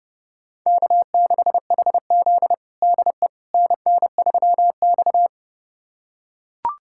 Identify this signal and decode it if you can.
Morse code, heard as a keyed tone, about 30 wpm, 700 Hz. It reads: K6HZ DE NN3X
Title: USA Callsign DE Callsign - 4 Characters - 30wpm